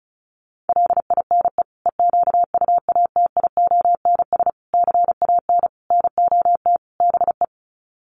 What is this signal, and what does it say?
Morse code, heard as a keyed tone, about 34 wpm, 700 Hz. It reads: LINE EQUATIONS CAN NOT BE
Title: Sets of 5 Words 35wpm